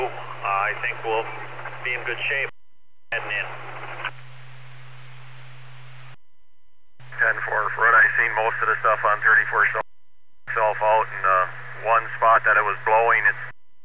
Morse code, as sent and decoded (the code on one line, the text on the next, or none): none